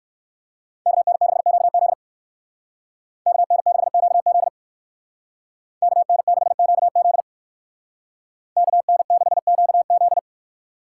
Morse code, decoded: KN6QZ KN6QZ KN6QZ KN6QZ

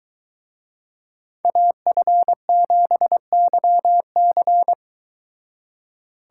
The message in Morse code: AF7YC

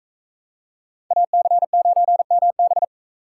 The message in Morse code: AC9MB

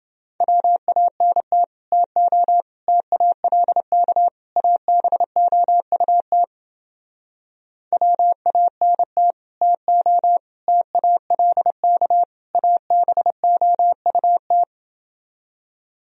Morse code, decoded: WANT TO TALK ABOUT WANT TO TALK ABOUT

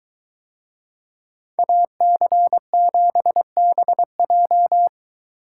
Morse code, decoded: AC7BJ

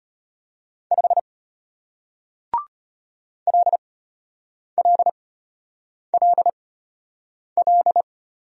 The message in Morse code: 5 LLLL

